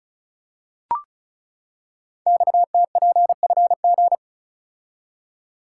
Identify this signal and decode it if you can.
Morse code, heard as a keyed tone, about 36 wpm, 700 Hz. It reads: XTPFG
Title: Code Groups - 5 Letters - 35wpm